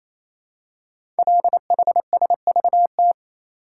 Morse code, heard as a keyed tone, about 27 wpm, 700 Hz. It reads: LHSVT
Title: Code Groups - 5 Letters - 28wpm